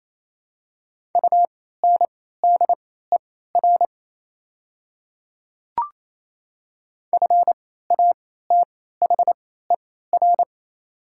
Morse code, heard as a keyed tone, about 28 wpm, spaced as for 16 wpm, 700 Hz. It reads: UNDER FATHER
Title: 3x Character Spacing - Top 200 Words - 28wpm